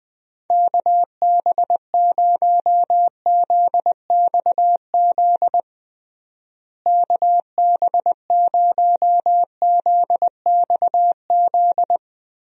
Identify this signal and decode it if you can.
Morse code, heard as a keyed tone, about 20 wpm, 700 Hz. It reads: KB0ZXZ KB0ZXZ
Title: USA Call Signs - 6 Characters - Repeated 3x - 20wpm